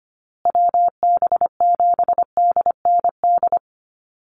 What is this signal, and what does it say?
Morse code, heard as a keyed tone, about 25 wpm, 700 Hz. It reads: WB7DND
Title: Call Signs Repeated 2x - 25wpm